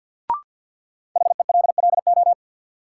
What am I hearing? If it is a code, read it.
Morse code, heard as a keyed tone, about 50 wpm, 700 Hz. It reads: HELLO
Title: Sentences from Top 1100 Words - New Word Drill - 50wpm